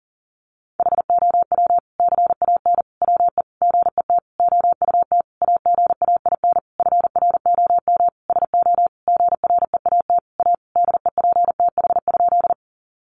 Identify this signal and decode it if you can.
Morse code, heard as a keyed tone, about 40 wpm, 700 Hz. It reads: HOW CAN WE GET OUT AGAIN FROM SO GREAT A DEPTH?
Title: Sentences from Top 1200 Words 40wpm